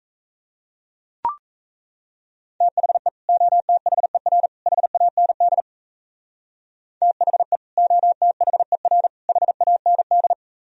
Morse code, heard as a keyed tone, about 34 wpm, 700 Hz. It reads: THE OTHER HAND THE OTHER HAND
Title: Speed-Racing - Sets of 3 Words 28wpm